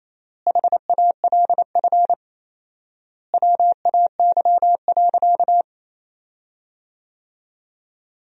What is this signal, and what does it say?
Morse code, heard as a keyed tone, about 28 wpm, 700 Hz. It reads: HALF WAY.
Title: 4x Word-Spacing - Sentences from Top 400 Words - 28wpm